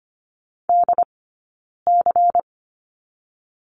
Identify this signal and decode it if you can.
Morse code, heard as a keyed tone, about 25 wpm, 700 Hz. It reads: D C
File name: course / 25:10wpm / 092 Lesson 16d - Letter c - Two Characters 25wpm